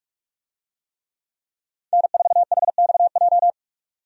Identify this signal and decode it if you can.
Morse code, heard as a keyed tone, about 43 wpm, 700 Hz. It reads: N4HXJ
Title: USA Call Signs - 5 Characters - 45wpm